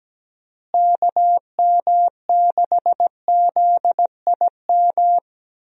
Morse code, heard as a keyed tone, about 17 wpm, 700 Hz. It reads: KM6ZIM
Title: USA Call Signs - 6 Characters - Repeated 2x - 17wpm